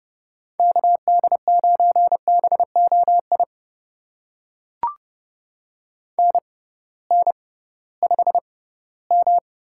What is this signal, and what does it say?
Morse code, heard as a keyed tone, about 30 wpm, spaced as for 18 wpm, 700 Hz. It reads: KD9BOI NN5M
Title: USA Call Signs - 6x Character Spacing - 30wpm